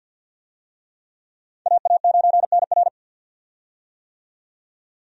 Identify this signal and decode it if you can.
Morse code, heard as a keyed tone, about 51 wpm, 700 Hz. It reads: AA9NR
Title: USA Call Signs - 5 Characters - 50wpm